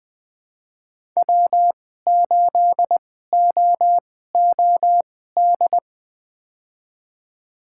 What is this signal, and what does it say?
Morse code, heard as a keyed tone, about 20 wpm, spaced as for 15 wpm, 700 Hz. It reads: W8OOD
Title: USA Call Signs - 5 Characters - 2x Character Spacing - 20wpm